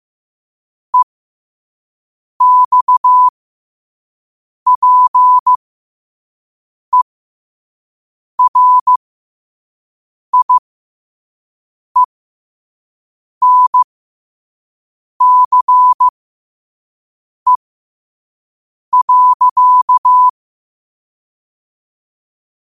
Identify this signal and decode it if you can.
Morse code, heard as a keyed tone, about 15 wpm, spaced as for 5 wpm, 1000 Hz. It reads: EXPERIENCE.
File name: qotdfiles / 5wpm